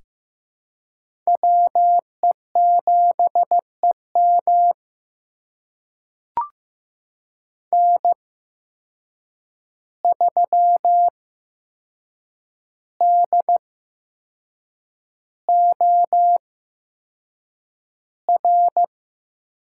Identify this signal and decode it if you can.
Morse code, heard as a keyed tone, about 15 wpm, spaced as for 6 wpm, 700 Hz. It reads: WE7EM N3DOR
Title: USA Call Signs - 5 Characters - 8x Character Spacing - 15wpm